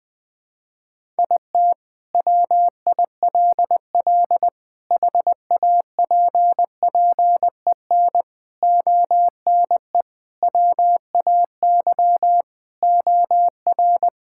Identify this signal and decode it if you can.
Morse code, heard as a keyed tone, about 20 wpm, 700 Hz. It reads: IT WILL HAPPEN ONE WAY OR
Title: Sentences with 5-Word Phrases - Spoken Beginning - 20wpm